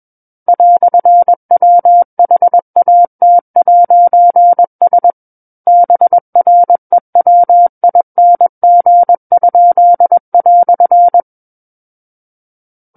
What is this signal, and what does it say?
Morse code, heard as a keyed tone, about 21 wpm, 700 Hz. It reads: "WHAT'S BREWING?"
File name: Book One / 12-Chapter 12 - 21 WPM